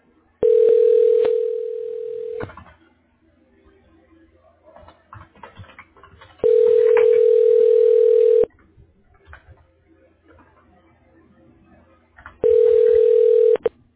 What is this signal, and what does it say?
Ringback tone (North America), 440 + 480 Hz.